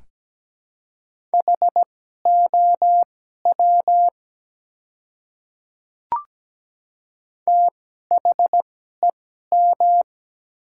Morse code, heard as a keyed tone, about 17 wpm, spaced as for 12 wpm, 700 Hz. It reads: HOW THEM
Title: 2x Character Spacing - Top 100 Words - 17wpm